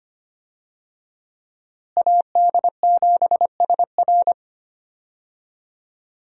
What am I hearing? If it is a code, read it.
Morse code, heard as a keyed tone, about 25 wpm, 700 Hz. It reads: AD7SR
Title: USA Call Signs - 5 Characters - Repeated 3x - 25wpm